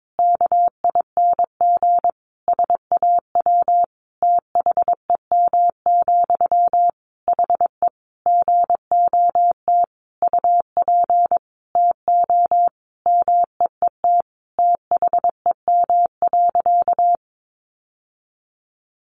Morse code, heard as a keyed tone, about 22 wpm, 700 Hz. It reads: KING SAW THEM, HE GOT UP TO MEET THEM.
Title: Sentences from Top 1100 Words 22wpm